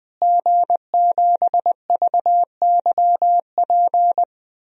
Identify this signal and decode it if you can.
Morse code, heard as a keyed tone, about 21 wpm, 700 Hz. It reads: G7VYP
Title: DXCC-100 Call Signs - 5 Characters - Repeated 2x - 20wpm